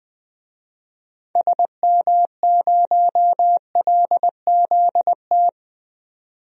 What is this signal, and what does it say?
Morse code, heard as a keyed tone, about 20 wpm, 700 Hz. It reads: SM0LZT
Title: DXCC-100 Call Signs - 6 Characters - 20wpm